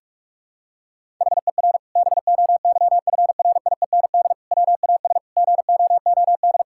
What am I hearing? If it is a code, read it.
Morse code, heard as a keyed tone, about 42 wpm, 700 Hz. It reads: HER BOYFRIEND WAS GOOD
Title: Sentences from Top 1000 Words - Review Part 5 - 45wpm